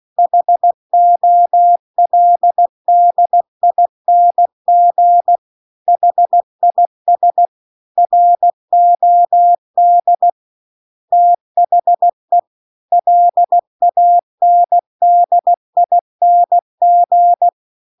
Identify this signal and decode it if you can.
Morse code, heard as a keyed tone, about 16 wpm, 700 Hz. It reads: HOLDING HIS ROD THE LANDING